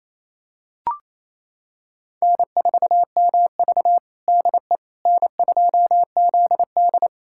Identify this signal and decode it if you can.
Morse code, heard as a keyed tone, about 28 wpm, 700 Hz. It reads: N4MV DE N2ZD
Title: USA Callsign DE Callsign - 4 Characters - 28wpm